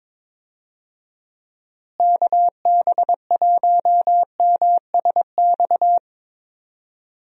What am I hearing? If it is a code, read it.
Morse code, heard as a keyed tone, about 22 wpm, 700 Hz. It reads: KB1MSX
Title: USA Call Signs - 6 Characters - Repeated 3x - 22wpm